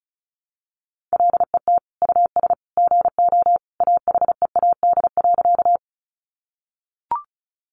Morse code, heard as a keyed tone, about 34 wpm, 700 Hz. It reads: LET US GO AHEAD.
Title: Sentences from Top 1100 Words 35wpm